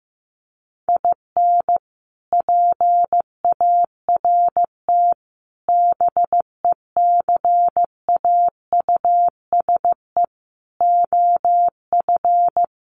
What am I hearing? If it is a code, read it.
Morse code, heard as a keyed tone, about 15 wpm, 700 Hz. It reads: IN PART BECAUSE OF
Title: Sets of 5 Words Encore2 15wpm